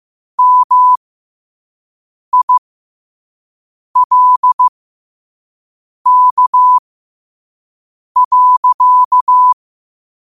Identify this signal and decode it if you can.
Morse code, heard as a keyed tone, about 15 wpm, spaced as for 5 wpm, 1000 Hz. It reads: MILK.
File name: qotdfiles / 5wpm